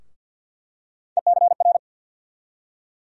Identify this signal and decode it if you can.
Morse code, heard as a keyed tone, about 46 wpm, 700 Hz. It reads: ECR
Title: USA Call Sign Postfixes - 3 Letters 50wpm